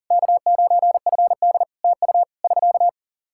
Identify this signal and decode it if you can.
Morse code, heard as a keyed tone, about 38 wpm, 700 Hz. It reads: K9FD TU <SK>
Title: Practice QSO - Exchange 4 - Ending - 40wpm